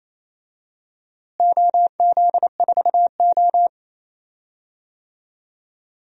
Morse code, heard as a keyed tone, about 28 wpm, 700 Hz. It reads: OZ4O
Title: DXCC-100 Call Signs - 4 Characters - Repeated 2x - 28wpm